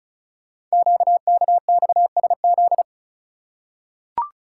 Morse code, heard as a keyed tone, about 36 wpm, 690 Hz. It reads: QKXSZ
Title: Code Groups - 5 Letters - 35wpm